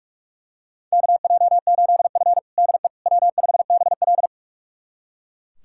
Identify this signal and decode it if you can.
Morse code, heard as a keyed tone, about 41 wpm, 700 Hz. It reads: KJ8F DE W5BL
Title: USA Callsign DE Callsign - 4 Characters - 45wpm